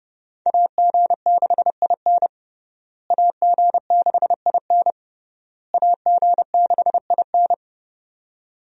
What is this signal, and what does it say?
Morse code, heard as a keyed tone, about 31 wpm, 700 Hz. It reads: AG6IN AG6IN AG6IN
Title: USA Call Signs - 5 Characters - Repeated 3x - 30wpm